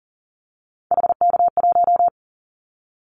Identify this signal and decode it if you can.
Morse code, heard as a keyed tone, about 40 wpm, 700 Hz. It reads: HK1
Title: Callsign Prefixes for Top 100 DXCC - 40wpm